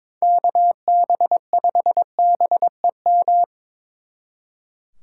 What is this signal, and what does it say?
Morse code, heard as a keyed tone, about 25 wpm, 700 Hz. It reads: KB5BEM
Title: Speed Racing - USA 2x3 Call Signs 22wpm